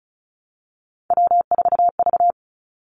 Morse code, heard as a keyed tone, about 36 wpm, 700 Hz. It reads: W4V
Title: Three Challenging Letter-Number Combinations 35wpm